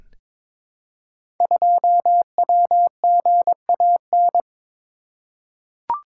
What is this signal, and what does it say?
Morse code, heard as a keyed tone, about 23 wpm, 700 Hz. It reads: 2WGAN
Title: Code Groups - 5 Letter:Numbers - 22wpm